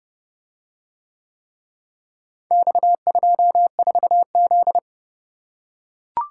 Morse code, heard as a keyed tone, about 30 wpm, 700 Hz. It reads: X24Z